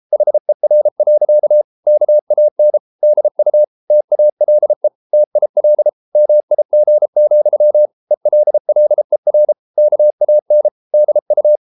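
Morse code, heard as a keyed tone, about 33 wpm, 600 Hz. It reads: HER. KAN DU TALE TIL MIG, ELLER KAN DU
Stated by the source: Morse code at 33 wpm, QRQ.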